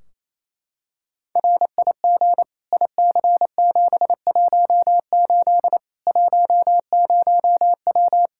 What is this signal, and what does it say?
Morse code, heard as a keyed tone, about 28 wpm, 700 Hz. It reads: RIG IC718 10W
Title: Practice QSO - Exchange 2 - Info - 28wpm